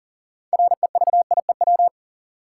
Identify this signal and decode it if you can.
Morse code, heard as a keyed tone, about 40 wpm, 700 Hz. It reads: REVIEW
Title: Six Letter Words - 40wpm